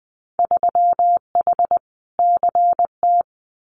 Morse code, heard as a keyed tone, about 20 wpm, 700 Hz. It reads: 3H CT
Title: Winter Field Day Exchanges Encore 20wpm